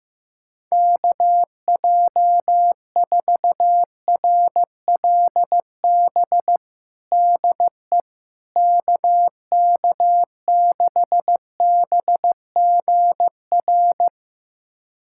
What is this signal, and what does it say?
Morse code, heard as a keyed tone, about 15 wpm, 700 Hz. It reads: KJ4RLB DE KK6BGR
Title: USA Callsign DE Callsign - 6 Characters - 15wpm